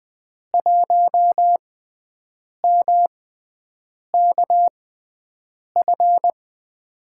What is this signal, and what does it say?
Morse code, heard as a keyed tone, about 21 wpm, spaced as for 7 wpm, 700 Hz. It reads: 1MKF
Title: USA Call Signs - 5 Characters - 6x Character Spacing - 20wpm